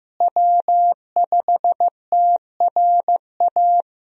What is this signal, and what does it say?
Morse code, heard as a keyed tone, about 15 wpm, 700 Hz. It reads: W5TRA
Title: USA Call Signs - 5 Characters - Repeated 2x - 15wpm